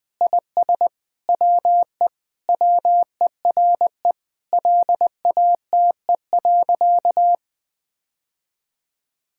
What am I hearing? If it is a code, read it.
Morse code, heard as a keyed tone, about 21 wpm, 700 Hz. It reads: IS WE WERE LATE.
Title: Sentences with 5-Word Phrases - 20wpm